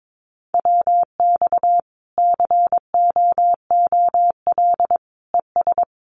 Morse code, heard as a keyed tone, about 22 wpm, 700 Hz. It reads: WX COOL ES
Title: Common QSO Phrases 22wpm